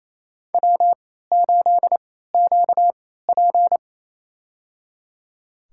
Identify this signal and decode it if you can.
Morse code, heard as a keyed tone, about 28 wpm, spaced as for 16 wpm, 700 Hz. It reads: W8QP
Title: USA Call Signs - 4 Characters - 3x Character Spacing - 28wpm